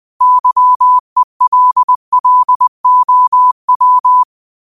Morse code, heard as a keyed tone, about 20 wpm, 1000 Hz. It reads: YELLOW